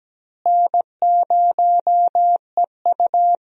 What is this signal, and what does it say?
Morse code, heard as a keyed tone, about 17 wpm, 700 Hz. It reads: N0EU
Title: USA Call Signs - 4 Characters - Repeated 3x - 17wpm